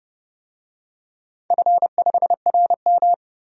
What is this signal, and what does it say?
Morse code, heard as a keyed tone, about 30 wpm, 700 Hz. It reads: F5RM
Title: DXCC-100 Call Signs - 4 Characters - 30wpm